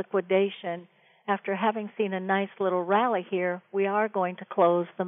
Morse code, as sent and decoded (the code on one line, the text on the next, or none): none